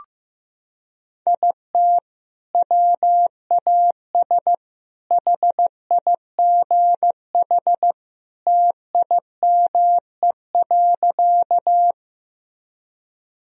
Morse code, heard as a keyed tone, about 15 wpm, 700 Hz. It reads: IT WAS HIGH TIME.
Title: Sentences from Top 1000 Words - Review Part 1 - 15wpm